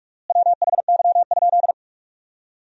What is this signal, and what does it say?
Morse code, heard as a keyed tone, about 43 wpm, 700 Hz. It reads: WHY?
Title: 4x Word-Spacing - Sentences from Top 200 Words - 45wpm